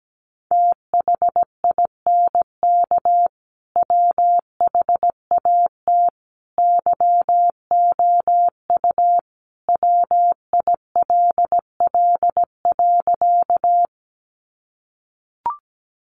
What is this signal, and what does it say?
Morse code, heard as a keyed tone, about 17 wpm, 700 Hz. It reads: THINK WHAT YOU WILL.
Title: Sentences from Top 200 Words 17wpm